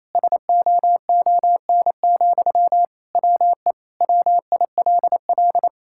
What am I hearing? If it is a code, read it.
Morse code, heard as a keyed tone, about 27 wpm, 700 Hz. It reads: SOON, WE WILL
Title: Sentences from Top 1000 Words - Review Part 6 - 28wpm